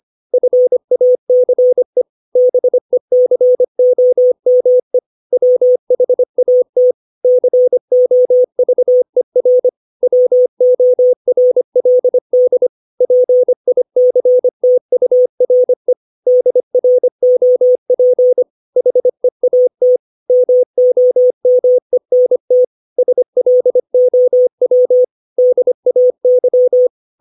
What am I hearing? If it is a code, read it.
Morse code, heard as a keyed tone, about 25 wpm, 500 Hz. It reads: FACE BECOME WHAT COVER WORLD PICTURE DROP HEAT MOMENT SLOW DAY
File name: 632 words / 632 Words 25WPM